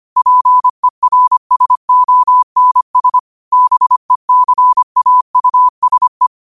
Morse code, heard as a keyed tone, about 25 wpm, 1000 Hz. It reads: PERSONS BECAUSE
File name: qotdfiles / 25wpm